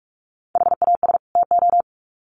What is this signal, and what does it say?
Morse code, heard as a keyed tone, about 44 wpm, 700 Hz. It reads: HAS TO
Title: Sets of 2 Words 45wpm